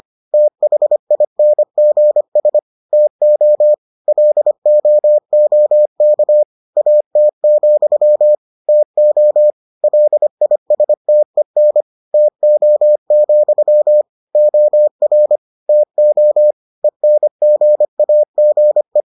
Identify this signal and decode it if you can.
Morse code, heard as a keyed tone, about 25 wpm, 600 Hz. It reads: THINGS TO LOOK AT, TO LISTEN TO, OR TO ENGAGE